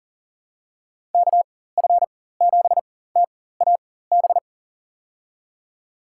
Morse code, heard as a keyed tone, about 40 wpm, spaced as for 19 wpm, 700 Hz. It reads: KF7TAB
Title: USA Call Signs - 4x Character Spacing - 40wpm